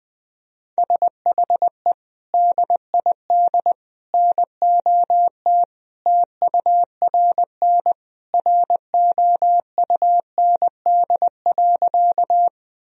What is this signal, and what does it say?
Morse code, heard as a keyed tone, about 20 wpm, 700 Hz. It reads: SHE DID NOT TURN ROUND.
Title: Sentences from Top 1000 Words - Review Part 1 - 20wpm